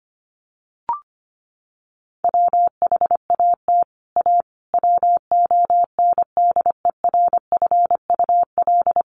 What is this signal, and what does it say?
Morse code, heard as a keyed tone, about 25 wpm, 700 Hz. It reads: WHAT A WONDERFUL
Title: Sentences from Top 1100 Words 25wpm